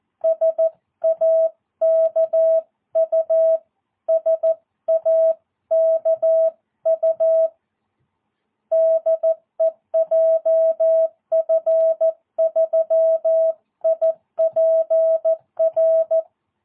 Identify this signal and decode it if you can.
Morse code, heard as a keyed tone, about 14 wpm, 650 Hz. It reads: SAKU SAKU DE JF3IPR
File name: jh8cbh / 2022cw